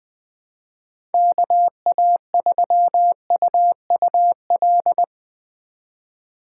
Morse code, heard as a keyed tone, about 20 wpm, 700 Hz. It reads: KA3UUL
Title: USA Call Signs - 6 Characters - 2x Character Spacing - 20wpm